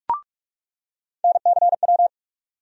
Morse code, heard as a keyed tone, about 47 wpm, 700 Hz. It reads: NCW
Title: USA Call Sign Postfixes - 3 Letters 45wpm